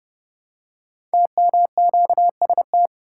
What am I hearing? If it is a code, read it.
Morse code, heard as a keyed tone, about 29 wpm, 700 Hz. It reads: TMQST
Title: Code Groups - 5 Letter:Numbers - 30wpm